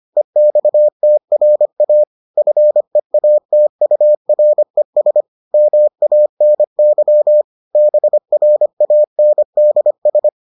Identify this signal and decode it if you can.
Morse code, heard as a keyed tone, about 25 wpm, 600 Hz. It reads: EXTRA FEATURES MANY BRANDS